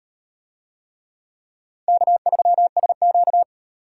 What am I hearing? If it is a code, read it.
Morse code, heard as a keyed tone, about 38 wpm, 700 Hz. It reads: K3SQ